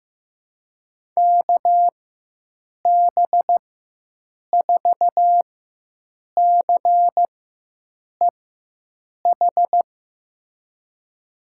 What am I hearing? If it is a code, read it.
Morse code, heard as a keyed tone, about 15 wpm, spaced as for 7 wpm, 700 Hz. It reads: KB4CEH
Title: USA Call Signs - 6 Characters - 4x Character Spacing - 15wpm